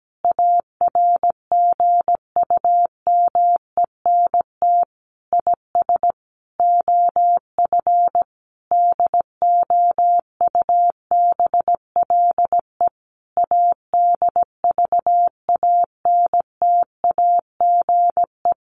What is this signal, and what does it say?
Morse code, heard as a keyed tone, about 17 wpm, 700 Hz. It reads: ARGUMENT IS OF DOUBLE ADVANTAGE
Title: Sentences from Top 1400 Words 17wpm